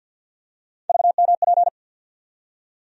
Morse code, heard as a keyed tone, about 52 wpm, 690 Hz. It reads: VMP
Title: USA Call Sign Postfixes - 3 Letters 50wpm